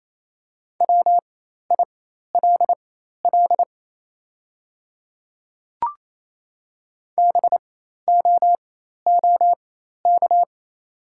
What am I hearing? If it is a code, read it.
Morse code, heard as a keyed tone, about 28 wpm, spaced as for 13 wpm, 700 Hz. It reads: WILL BOOK
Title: 4x Character Spacing - Top 300 Words - 28wpm